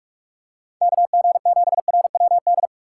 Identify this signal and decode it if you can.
Morse code, heard as a keyed tone, about 46 wpm, 700 Hz. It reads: KG7RWD
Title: USA Call Signs - 6 Characters - 3x Character Spacing - 45wpm